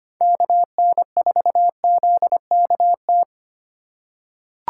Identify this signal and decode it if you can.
Morse code, heard as a keyed tone, about 25 wpm, 700 Hz. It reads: KN4ZKT
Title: K1USN SST QSO Practice - 25wpm